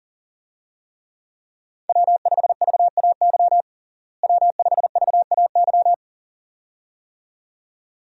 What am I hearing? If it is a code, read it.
Morse code, heard as a keyed tone, about 40 wpm, 700 Hz. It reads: W5VAY W5VAY
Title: USA Call Signs - 5 Characters - Repeated 2x - 40wpm